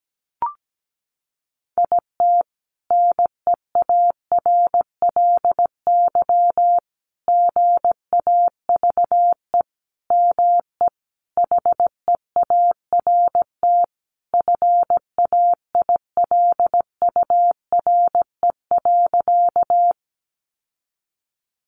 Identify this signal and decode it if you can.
Morse code, heard as a keyed tone, about 17 wpm, 700 Hz. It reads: IT NEARLY GAVE ME HEART FAILURE.
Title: Sentences from Top 1300 Words 17wpm